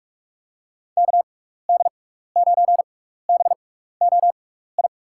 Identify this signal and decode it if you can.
Morse code, heard as a keyed tone, about 46 wpm, spaced as for 15 wpm, 700 Hz. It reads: KD9BOI
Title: USA Call Signs - 6 Characters - 6x Character Spacing - 45wpm